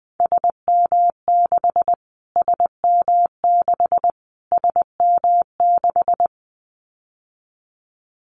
Morse code, heard as a keyed tone, about 20 wpm, 700 Hz. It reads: SM6 SM6 SM6
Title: Callsign Prefixes for Top 25 DXCC - Repeated 3x - 20wpm